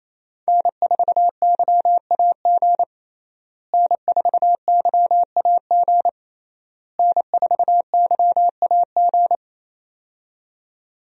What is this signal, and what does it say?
Morse code, heard as a keyed tone, about 27 wpm, 700 Hz. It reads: N4YAG N4YAG N4YAG
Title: USA Call Signs - 5 Characters - Repeated 3x - 28wpm